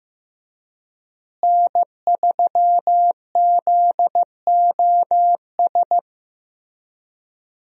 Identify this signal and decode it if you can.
Morse code, heard as a keyed tone, about 15 wpm, 700 Hz. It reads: N3ZOS
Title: USA Call Signs - 5 Characters - Repeated 2x - 15wpm